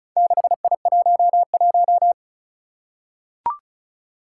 Morse code, heard as a keyed tone, about 36 wpm, 700 Hz. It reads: 6I11
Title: Four Challenging Letter-Number Combinations 35wpm